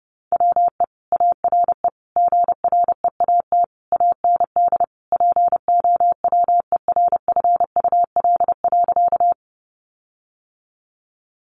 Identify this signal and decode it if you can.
Morse code, heard as a keyed tone, about 30 wpm, 700 Hz. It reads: WE ARE GREAT AND POWERFUL.
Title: Sentences from Top 1300 Words 30wpm